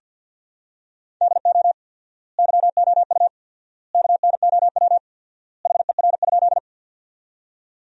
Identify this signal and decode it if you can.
Morse code, heard as a keyed tone, about 47 wpm, 700 Hz. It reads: DO YOU KNOW HER?
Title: 4x Word-Spacing - Sentences from Top 100 Words - 50wpm